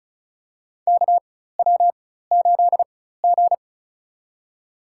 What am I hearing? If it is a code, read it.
Morse code, heard as a keyed tone, about 36 wpm, spaced as for 16 wpm, 700 Hz. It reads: KW8G
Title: USA Call Signs - 4 Characters - 4x Character Spacing - 35wpm